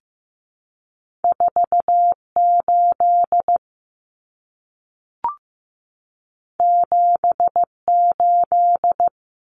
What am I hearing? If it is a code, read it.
Morse code, heard as a keyed tone, about 15 wpm, 700 Hz. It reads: 48 78
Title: Common Cut Numbers - 2-Digit 15wpm